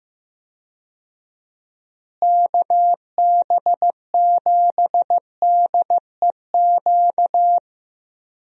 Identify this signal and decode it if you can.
Morse code, heard as a keyed tone, about 15 wpm, 700 Hz. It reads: KB7DEQ
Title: USA Call Signs - 6 Characters - Repeated 2x - 15wpm